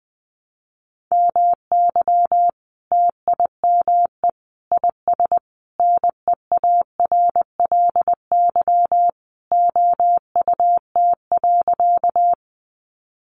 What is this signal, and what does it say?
Morse code, heard as a keyed tone, about 20 wpm, 700 Hz. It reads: MY TIME IS NEARLY OUT.
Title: Sentences from Top 1300 Words 20wpm